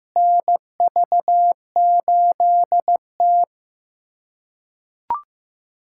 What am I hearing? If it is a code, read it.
Morse code, heard as a keyed tone, about 15 wpm, 700 Hz. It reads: NV8T
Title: USA Callsign DE Callsign - 4 Characters - 15wpm